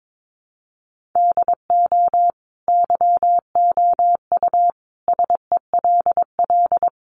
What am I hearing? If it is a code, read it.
Morse code, heard as a keyed tone, about 22 wpm, 700 Hz. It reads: DO YOU SELL
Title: Sentences from Top 1400 Words 22wpm